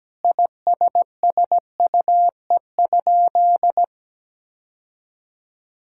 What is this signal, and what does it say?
Morse code, heard as a keyed tone, about 17 wpm, 700 Hz. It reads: ISSUE?
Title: Sentences from Top 1000 Words - Review Part 6 - 17wpm